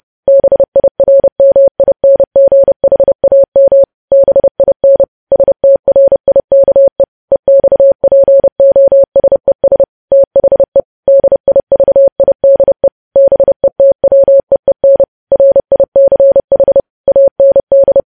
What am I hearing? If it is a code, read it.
Morse code, heard as a keyed tone, about 30 wpm, 550 Hz. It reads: BIRMINGHAM BIN STRIKE EXPOSES THE DIVIDE BETWEEN RICH AND
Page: news headlines in morse code at 30 wpm